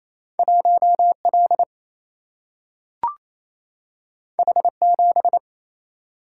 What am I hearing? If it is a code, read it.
Morse code, heard as a keyed tone, about 28 wpm, 700 Hz. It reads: 1L H7
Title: Single-Letter Single-Number 28wpm